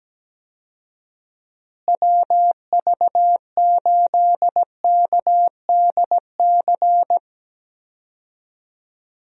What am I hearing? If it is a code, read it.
Morse code, heard as a keyed tone, about 17 wpm, 700 Hz. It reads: WV8KDC
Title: USA Call Signs - 6 Characters - Repeated 2x - 17wpm